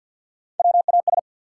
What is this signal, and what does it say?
Morse code, heard as a keyed tone, about 53 wpm, 700 Hz. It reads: WAS
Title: Three Letter Words - 50wpm